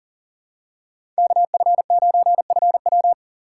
Morse code, heard as a keyed tone, about 40 wpm, 700 Hz. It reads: KF9FW
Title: USA Call Signs - 5 Characters - 2x Character Spacing - 40wpm